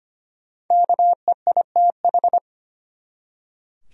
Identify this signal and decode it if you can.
Morse code, heard as a keyed tone, about 25 wpm, 700 Hz. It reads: KEITH
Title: USA Common Names - 25wpm